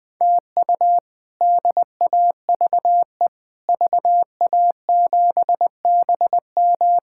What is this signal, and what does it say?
Morse code, heard as a keyed tone, about 21 wpm, 700 Hz. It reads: TU DAVE VA7BM
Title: K1USN SST QSO Practice - 20wpm